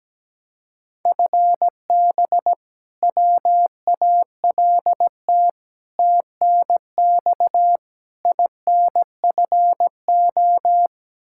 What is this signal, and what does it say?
Morse code, heard as a keyed tone, about 17 wpm, 700 Hz. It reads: FB WALT TNX INFO
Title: Practice QSO - Exchange 3 - More Info - 17wpm